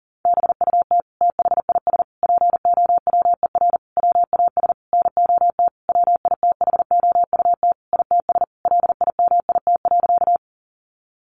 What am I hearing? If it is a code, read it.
Morse code, heard as a keyed tone, about 40 wpm, 700 Hz. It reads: BUT THIS POWER WAS NOT WITHOUT ITS LIMIT.
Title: Sentences from Top 1500 Words 40wpm